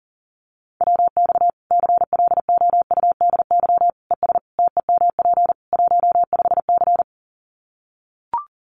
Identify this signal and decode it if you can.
Morse code, heard as a keyed tone, about 40 wpm, 700 Hz. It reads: WX CLOUDY ES TEMP 15C
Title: Common QSO Phrases 40wpm